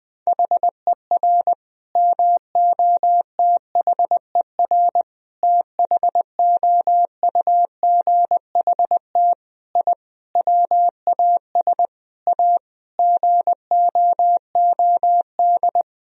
Morse code, heard as a keyed tone, about 21 wpm, 700 Hz. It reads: HER MOTHER THOUGHT I WAS A GOOD
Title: Sentences from Top 1000 Words - Review Part 4 - 20wpm